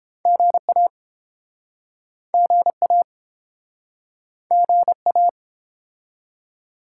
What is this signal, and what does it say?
Morse code, heard as a keyed tone, about 32 wpm, 700 Hz. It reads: GA GA GA